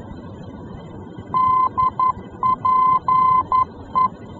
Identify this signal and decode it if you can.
Morse code, heard as a keyed tone, about 11 wpm, 1030 Hz. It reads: DPE
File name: france / mx / DPE-F-VOR